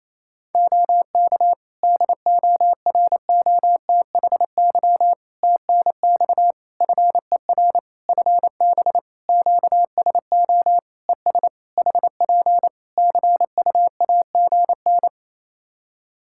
Morse code, heard as a keyed tone, about 28 wpm, 700 Hz. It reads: OK DOROTHY TNX FER FB QSO ES HP CUAGN
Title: Practice QSO - Exchange 4 - Ending - 28wpm